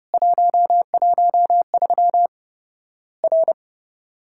Morse code, heard as a keyed tone, about 30 wpm, 700 Hz. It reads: 113 R
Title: Worked All Europe DX Contest - QTC Exchange - 30wpm